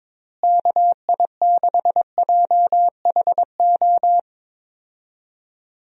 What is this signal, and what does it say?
Morse code, heard as a keyed tone, about 21 wpm, 700 Hz. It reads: KI6JHO
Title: USA Call Signs - 6 Characters - Repeated 2x - 22wpm